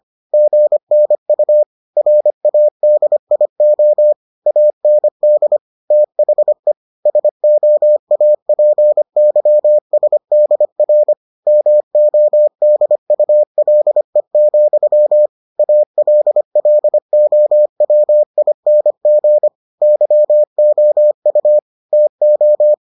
Morse code, heard as a keyed tone, about 25 wpm, 600 Hz. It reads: GNU RADIO AND THE SOAPYSDR MODULE, ALLOWING YOU TO